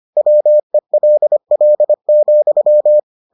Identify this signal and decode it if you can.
Morse code, heard as a keyed tone, about 25 wpm, 600 Hz. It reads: WELL,